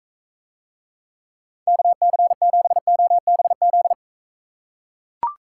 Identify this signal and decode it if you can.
Morse code, heard as a keyed tone, about 43 wpm, 700 Hz. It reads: KC7OBZ